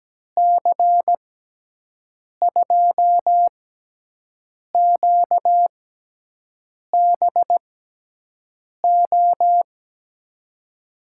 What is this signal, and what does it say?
Morse code, heard as a keyed tone, about 17 wpm, 700 Hz. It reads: C 2 Q B O
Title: USA Call Signs - 6 Characters - 6x Character Spacing - 17wpm